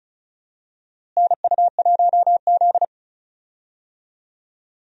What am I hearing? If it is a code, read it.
Morse code, heard as a keyed tone, about 34 wpm, 700 Hz. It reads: NU1Z
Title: USA Call Signs - 4 Characters - Repeated 2x - 35wpm